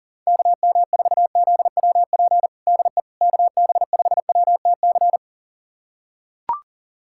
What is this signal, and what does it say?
Morse code, heard as a keyed tone, about 40 wpm, 700 Hz. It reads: KM4ZWP DE KB5WTC
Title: USA Callsign DE Callsign 40wpm